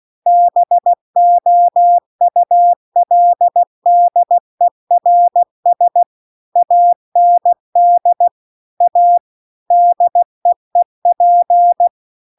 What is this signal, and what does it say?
Morse code, heard as a keyed tone, about 16 wpm, 700 Hz. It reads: BOULDERS AND A DEEP